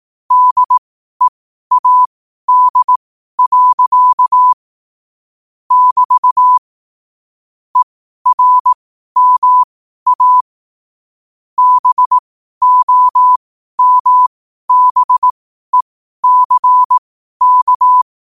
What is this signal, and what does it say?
Morse code, heard as a keyed tone, about 18 wpm, spaced as for 12 wpm, 1000 Hz. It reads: DEAD. <BT> ERMA BOMBECK
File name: qotdfiles / 12wpm